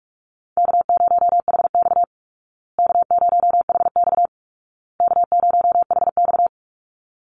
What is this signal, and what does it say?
Morse code, heard as a keyed tone, about 44 wpm, 700 Hz. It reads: K0HX K0HX K0HX
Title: Call Signs Repeated 2x - 45wpm